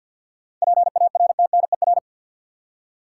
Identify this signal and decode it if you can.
Morse code, heard as a keyed tone, about 53 wpm, 700 Hz. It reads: PARTNER
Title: Seven Letter Words - 50wpm